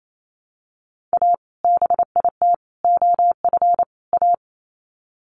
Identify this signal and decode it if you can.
Morse code, heard as a keyed tone, about 28 wpm, 700 Hz. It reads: A BIT OF A
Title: Sets of 4 Words 28wpm